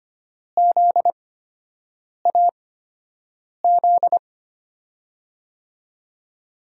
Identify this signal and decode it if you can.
Morse code, heard as a keyed tone, about 25 wpm, spaced as for 7 wpm, 700 Hz. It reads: ZAZ
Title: USA Call Signs - 6 Characters - 8x Character Spacing - 25wpm